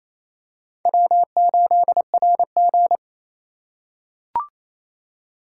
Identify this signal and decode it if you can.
Morse code, heard as a keyed tone, about 27 wpm, 700 Hz. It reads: W8RG
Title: USA Call Signs - 4 Characters - 4x Character Spacing - 28wpm